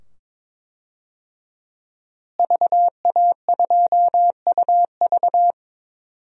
Morse code, heard as a keyed tone, about 22 wpm, 700 Hz. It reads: VA2UV
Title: DXCC-100 Call Signs - 5 Characters - Repeated 2x - 22wpm